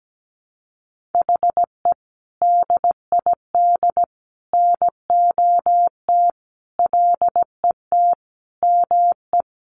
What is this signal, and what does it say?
Morse code, heard as a keyed tone, about 17 wpm, 700 Hz. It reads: HE DID NOT LET ME
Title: Sentences from Top 1200 Words 17wpm